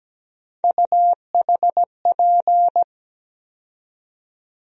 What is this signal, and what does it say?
Morse code, heard as a keyed tone, about 17 wpm, 700 Hz. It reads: UHP
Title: USA Call Sign Postfixes - 3 Letters 17wpm